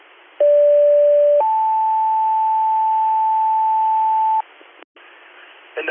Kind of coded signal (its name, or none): two-tone page